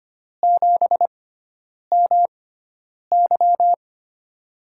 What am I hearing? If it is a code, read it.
Morse code, heard as a keyed tone, about 25 wpm, spaced as for 9 wpm, 700 Hz. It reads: 7MY
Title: USA Call Signs - 6x Character Spacing - 25wpm